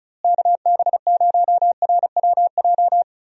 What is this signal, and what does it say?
Morse code, heard as a keyed tone, about 36 wpm, 700 Hz. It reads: KB0RWJ
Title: USA Call Signs - 6 Characters - Repeated 2x - 35wpm